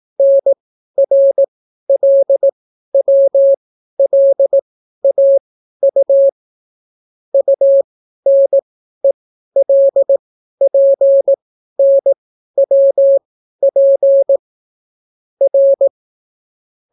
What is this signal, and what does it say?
Morse code, heard as a keyed tone, about 18 wpm, spaced as for 12 wpm, 550 Hz. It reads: NRLWLAU UNELPNWP R